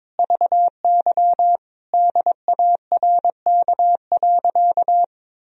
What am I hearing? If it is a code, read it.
Morse code, heard as a keyed tone, about 22 wpm, 700 Hz. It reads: VY DARK.
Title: Sentences from Top 500 Words and Abbreviations 22wpm